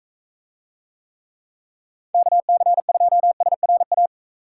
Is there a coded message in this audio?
Yes: Morse code from 2.1 to 4.1 s, heard as a keyed tone, about 41 wpm, 700 Hz. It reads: KC2SRA